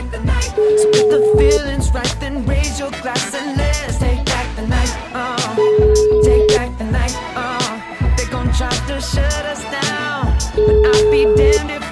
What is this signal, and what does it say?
Ringback tone (Europe, 425 Hz), with noise.